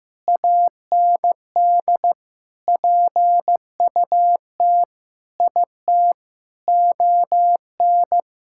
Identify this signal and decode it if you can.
Morse code, heard as a keyed tone, about 15 wpm, 700 Hz. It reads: AND PUT IT ON
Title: Sentences from Top 1600 Words 15wpm